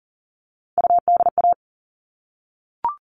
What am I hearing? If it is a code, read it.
Morse code, heard as a keyed tone, about 40 wpm, 700 Hz. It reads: UDA
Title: All Cut Numbers - 3-Digit 40wpm